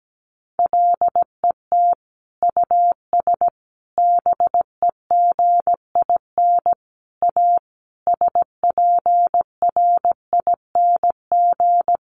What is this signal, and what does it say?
Morse code, heard as a keyed tone, about 17 wpm, 700 Hz. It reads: LET US BEGIN A SPRING
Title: Sentences from Top 1400 Words 17wpm